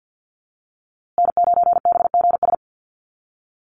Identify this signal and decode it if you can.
Morse code, heard as a keyed tone, about 50 wpm, 700 Hz. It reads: N9BGS
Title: USA Call Signs Repeated 3x 50wpm